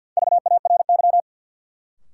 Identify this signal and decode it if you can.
Morse code, heard as a keyed tone, about 52 wpm, 700 Hz. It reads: VARY
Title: Sentences from Top 1200 Words - New Word Drill - 50wpm